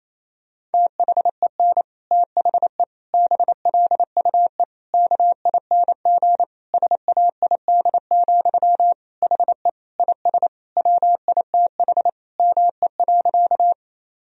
Morse code, heard as a keyed tone, about 28 wpm, 700 Hz. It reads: THEN THE BLUE KING SAID, HE IS WITH ME.
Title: Sentences from Top 1000 Words - Review Part 3 - 28wpm